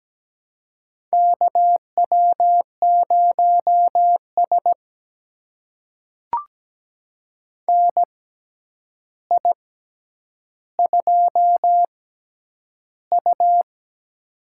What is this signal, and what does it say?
Morse code, heard as a keyed tone, about 17 wpm, spaced as for 9 wpm, 700 Hz. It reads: KW0S NI2U
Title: USA Call Signs - 4 Characters - 6x Character Spacing - 17wpm